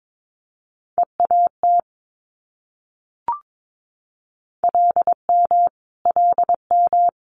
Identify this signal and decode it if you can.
Morse code, heard as a keyed tone, about 22 wpm, 700 Hz. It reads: EAT LM LM
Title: Multi-letter Phonemes Repeated 3x 22wpm